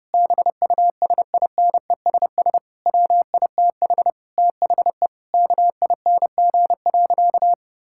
Morse code, heard as a keyed tone, about 32 wpm, 700 Hz. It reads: BUSINESS WITH THE KING.
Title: Sentences from Top 1000 Words - Review Part 6 - 30wpm